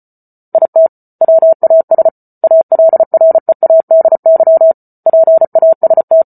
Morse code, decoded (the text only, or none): IT WAS ALREADY PAST